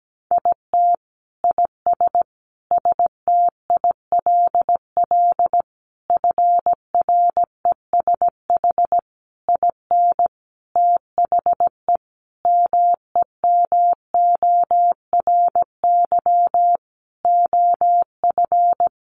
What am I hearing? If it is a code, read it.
Morse code, heard as a keyed tone, about 17 wpm, 700 Hz. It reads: IT IS STILL FRESH IN THE MEMORY OF
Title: Sentences from Top 1300 Words 17wpm